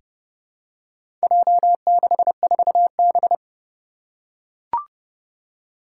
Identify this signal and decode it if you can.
Morse code, heard as a keyed tone, about 30 wpm, 700 Hz. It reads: J64B
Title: Four Challenging Letter-Number Combinations 30wpm